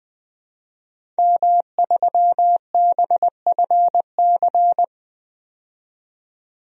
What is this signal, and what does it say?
Morse code, heard as a keyed tone, about 20 wpm, 700 Hz. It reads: M3BFC
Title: DXCC-100 Call Signs - 5 Characters - 20wpm